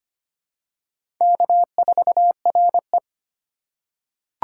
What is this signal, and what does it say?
Morse code, heard as a keyed tone, about 25 wpm, 700 Hz. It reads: K4RE
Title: USA Call Signs - 2x Character Spacing - 25wpm